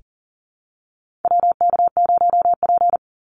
Morse code, heard as a keyed tone, about 40 wpm, 700 Hz. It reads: WK0P
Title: Call Signs - 40wpm